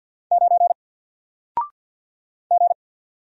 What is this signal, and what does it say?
Morse code, heard as a keyed tone, about 50 wpm, 700 Hz. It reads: 9 G
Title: Single Letter-Number Alt 50wpm